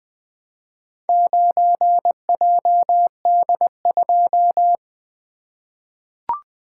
Morse code, heard as a keyed tone, about 20 wpm, 700 Hz. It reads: 9JD2
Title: Four Letter-Number Combinations - 20wpm